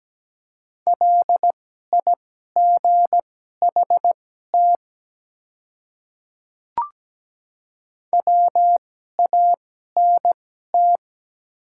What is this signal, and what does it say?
Morse code, heard as a keyed tone, about 17 wpm, spaced as for 12 wpm, 700 Hz. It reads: LIGHT WANT
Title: 2x Character Spacing - Top 200 Words - 17wpm